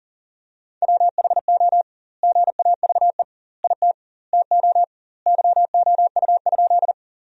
Morse code, heard as a keyed tone, about 40 wpm, 700 Hz. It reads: WHO GAVE IT TO YOU?
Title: 2x Word-Spacing - Sentences from Top 500 Words - 40wpm